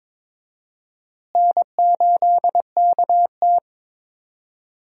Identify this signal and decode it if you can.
Morse code, heard as a keyed tone, about 23 wpm, 700 Hz. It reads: N8KT
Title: USA Call Signs - 4 Characters - Repeated 2x - 22wpm